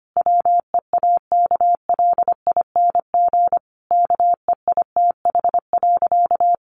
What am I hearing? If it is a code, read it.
Morse code, heard as a keyed tone, about 25 wpm, 700 Hz. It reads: WEAKLING KEITH.
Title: A Sound Of Thunder 25wpm